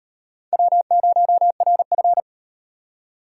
Morse code, heard as a keyed tone, about 35 wpm, 700 Hz. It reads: W0RF